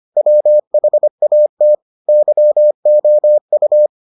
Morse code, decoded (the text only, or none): WHAT YOU